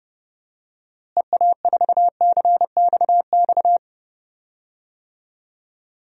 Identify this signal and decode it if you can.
Morse code, heard as a keyed tone, about 30 wpm, 700 Hz. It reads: EA4CXX
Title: DXCC-100 Call Signs - 6 Characters - Repeated 3x - 30wpm